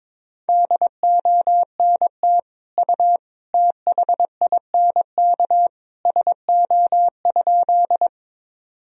Morse code, heard as a keyed tone, about 22 wpm, 700 Hz. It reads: DONT U THINK SO?
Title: Sentences from Top 300 Words and Abbreviations 22wpm